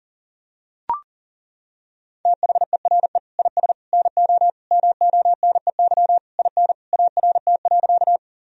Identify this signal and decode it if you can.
Morse code, heard as a keyed tone, about 40 wpm, 700 Hz. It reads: THERE IS NO MONEY IN ART.
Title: Sentences from Top 1000 Words - Review Part 3 - 40wpm